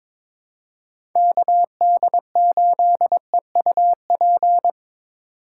Morse code, heard as a keyed tone, about 26 wpm, 700 Hz. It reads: KD8EUP
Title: Speed Racing - USA 2x3 Call Signs 22wpm